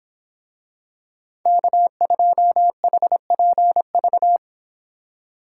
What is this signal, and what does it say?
Morse code, heard as a keyed tone, about 26 wpm, 700 Hz. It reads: K2HPV